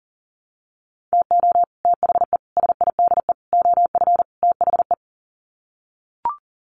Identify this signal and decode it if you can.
Morse code, heard as a keyed tone, about 40 wpm, 700 Hz. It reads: TO THE SIDE OF THE
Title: Sets of 5 Words 40wpm